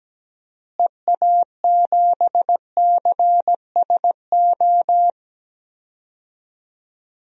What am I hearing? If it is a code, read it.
Morse code, heard as a keyed tone, about 17 wpm, 700 Hz. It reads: EA7CSO
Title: DXCC-100 Call Signs - 6 Characters - Repeated 3x - 17wpm